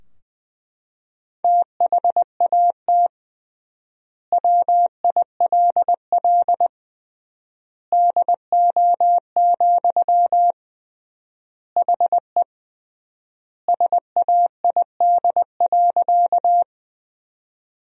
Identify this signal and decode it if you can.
Morse code, heard as a keyed tone, about 21 wpm, 700 Hz. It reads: THAT WILL DO, HE SAID.
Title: 3x Word-Spacing - Sentences from Top 100 Words - 20wpm